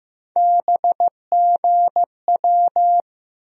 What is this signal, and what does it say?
Morse code, heard as a keyed tone, about 15 wpm, 700 Hz. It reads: BGW
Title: USA Call Sign Postfixes - 3 Letters 15wpm